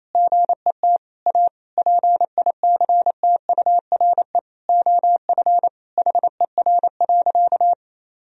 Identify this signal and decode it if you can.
Morse code, heard as a keyed tone, about 28 wpm, 700 Hz. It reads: GET A PICTURE OF HER.
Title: Sentences from Top 1000 Words - Review Part 1 - 28wpm